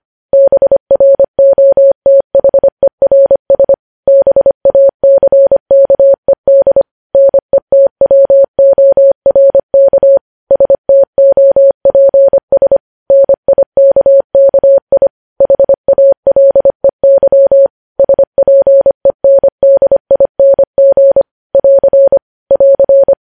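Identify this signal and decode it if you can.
Morse code, heard as a keyed tone, about 25 wpm, 550 Hz. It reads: BROTHERS BACKED NETWORK STOPS NIKKI HALEY SPENDING <AR> <AR>
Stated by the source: News headlines sent at 25.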